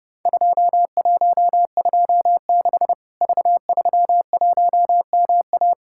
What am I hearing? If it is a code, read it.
Morse code, heard as a keyed tone, about 30 wpm, 700 Hz. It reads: 2126 V31MA
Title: Worked All Europe DX Contest - QTC Exchange - 30wpm